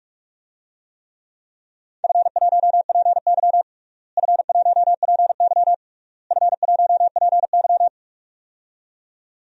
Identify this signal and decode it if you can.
Morse code, heard as a keyed tone, about 46 wpm, 700 Hz. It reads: F1PY F1PY F1PY
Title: DXCC-100 Call Signs - 4 Characters - Repeated 3x - 45wpm